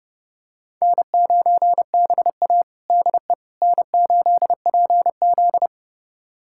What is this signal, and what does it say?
Morse code, heard as a keyed tone, about 30 wpm, 700 Hz. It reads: N9BA DE N8PZ
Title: USA Callsign DE Callsign - 4 Characters - 30wpm